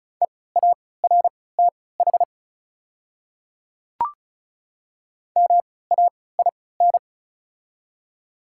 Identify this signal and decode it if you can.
Morse code, heard as a keyed tone, about 34 wpm, spaced as for 20 wpm, 700 Hz. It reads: EARTH MAIN
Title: 3x Character Spacing - Top 300 Words - 35wpm